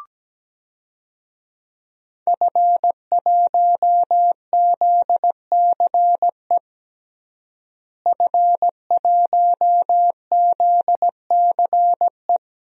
Morse code, heard as a keyed tone, about 17 wpm, 700 Hz. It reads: F1ZCE F1ZCE
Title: DXCC-100 Call Signs - 5 Characters - Repeated 2x - 17wpm